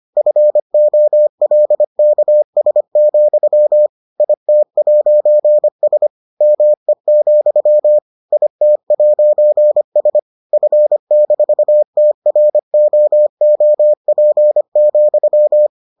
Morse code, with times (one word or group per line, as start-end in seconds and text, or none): 0.2-3.9 FOLKS,
4.2-6.1 IT'S
6.4-8.0 ME,
8.3-10.2 IT'S
10.5-15.7 F-TROOP,